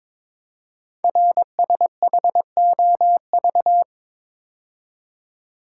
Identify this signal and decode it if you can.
Morse code, heard as a keyed tone, about 22 wpm, 700 Hz. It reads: RSHOV
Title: Code Groups - 5 Letter:Numbers - 22wpm